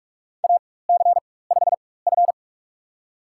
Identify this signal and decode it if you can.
Morse code, heard as a keyed tone, about 46 wpm, spaced as for 21 wpm, 700 Hz. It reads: AC5F
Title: USA Call Signs - 4 Characters - 4x Character Spacing - 45wpm